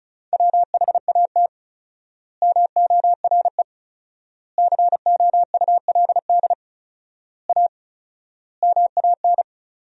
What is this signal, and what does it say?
Morse code, heard as a keyed tone, about 36 wpm, 700 Hz. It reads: WHAT MORE COULD A MAN
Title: 4x Word-Spacing - Sentences from Top 200 Words - 35wpm